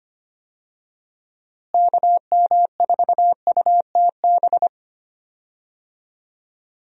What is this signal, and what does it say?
Morse code, heard as a keyed tone, about 25 wpm, 700 Hz. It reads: KM4UTB
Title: USA Call Signs - 6 Characters - Repeated 2x - 25wpm